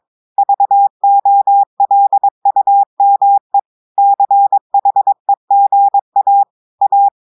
Morse code, heard as a keyed tone, about 22 wpm, 800 Hz. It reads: VOLUME CHEGA A
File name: audios / portugais / portugais texte en clair 1 vitesse 22 mots minutes